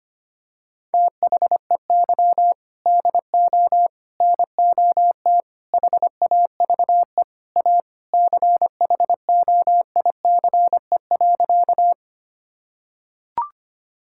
Morse code, heard as a keyed tone, about 25 wpm, 700 Hz. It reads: THEY DO NOT HAVE A CHOICE.
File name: practice / 25wpm / Sentences with 4-Word Phrases - 25wpm